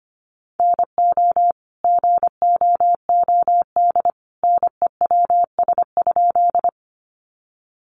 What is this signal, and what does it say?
Morse code, heard as a keyed tone, about 25 wpm, 700 Hz. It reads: NO GOOD NEWS?
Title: Sentences from Top 800 Words 25wpm